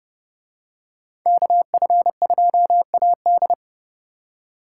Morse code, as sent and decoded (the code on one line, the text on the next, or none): -.- ..-. ..--- .- -..
KF2AD